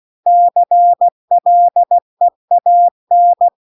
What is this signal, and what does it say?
Morse code, heard as a keyed tone, about 16 wpm, 700 Hz. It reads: CLEAN
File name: CW-The-Big-Twoheaded-River-2-16WPM